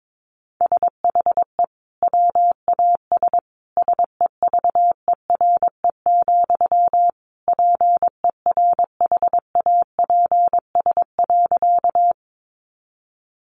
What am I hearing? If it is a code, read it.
Morse code, heard as a keyed tone, about 22 wpm, 700 Hz. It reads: SHE WAS SEVERE, PERHAPS.
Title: Sentences from Top 1400 Words 22wpm